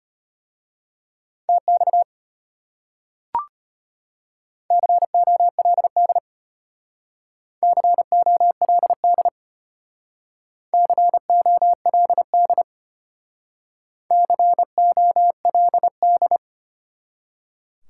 Morse code, heard as a keyed tone, about 34 wpm, 700 Hz. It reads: TX COLD COLD COLD COLD